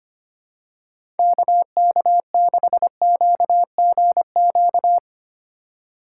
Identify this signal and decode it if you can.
Morse code, heard as a keyed tone, about 25 wpm, 700 Hz. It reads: KK6QGQ
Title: USA Call Signs - 6 Characters - 3x Character Spacing - 25wpm